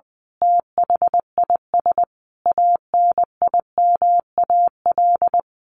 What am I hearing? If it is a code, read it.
Morse code, heard as a keyed tone, about 20 wpm, 700 Hz. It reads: THIS ANIMAL
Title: A Sound Of Thunder 20wpm